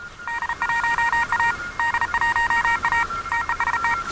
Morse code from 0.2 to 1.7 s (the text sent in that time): D1A